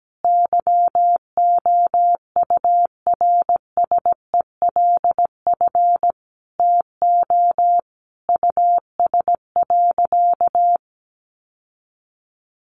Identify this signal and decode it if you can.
Morse code, heard as a keyed tone, about 17 wpm, 700 Hz. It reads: YOURSELF TO US.
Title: Sentences from Top 600 Words 17wpm